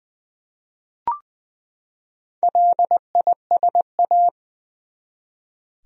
Morse code, heard as a keyed tone, about 20 wpm, 700 Hz. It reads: LISA
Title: USA Common Names - Max 5 Letters - 20wpm